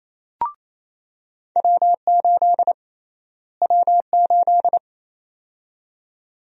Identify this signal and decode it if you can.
Morse code, heard as a keyed tone, about 27 wpm, 700 Hz. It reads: W8 W8
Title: USA Call Sign Prefixes - Repeated 2x - 28wpm